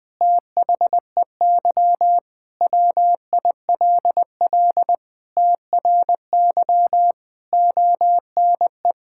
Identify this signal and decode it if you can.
Morse code, heard as a keyed tone, about 21 wpm, 700 Hz. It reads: THEY WILL TRY ONE
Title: Sentences from Top 1000 Words - Review Part 6 - 20wpm